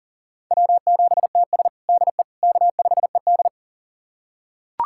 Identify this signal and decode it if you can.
Morse code, heard as a keyed tone, about 40 wpm, 700 Hz. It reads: W7TS DE K5ED
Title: USA Callsign DE Callsign - 4 Characters - 40wpm